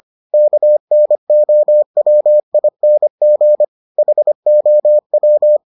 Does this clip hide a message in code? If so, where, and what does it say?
Yes: Morse code from 0.3 to 5.6 s, heard as a keyed tone, about 25 wpm, 600 Hz. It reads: KNOWING HOW